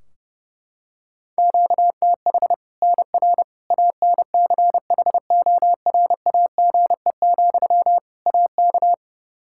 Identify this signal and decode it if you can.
Morse code, heard as a keyed tone, about 32 wpm, 700 Hz. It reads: QTH NR ANCHORAGE, AK
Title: K1USN SST QSO Practice 30wpm